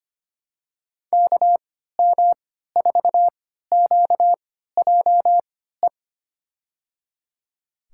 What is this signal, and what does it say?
Morse code, heard as a keyed tone, about 25 wpm, spaced as for 14 wpm, 700 Hz. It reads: KM4QJE
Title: USA Call Signs - 6 Characters - 3x Character Spacing - 25wpm